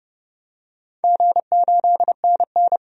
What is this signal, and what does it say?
Morse code, heard as a keyed tone, about 30 wpm, 700 Hz. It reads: G8NN
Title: DXCC-100 Call Signs - 4 Characters - 30wpm